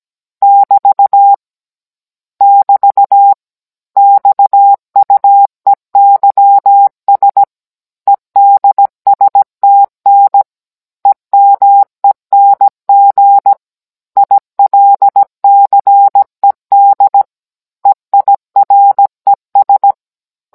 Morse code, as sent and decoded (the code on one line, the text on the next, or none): -...- / -...- / -..- ..- . -.-- ... / . -.. ... - -. / . -- . -. --. / .. .-.. -.-. -.. / . .. .-. ...
<BT> <BT> XUEYS EDSTN EMENG ILCED EIRES